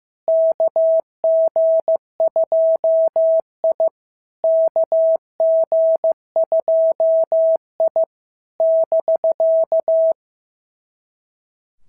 Morse code, heard as a keyed tone, about 14 wpm, 650 Hz. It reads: KG2I KG2I <BK>
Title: SOTA QSO Practice 15wpm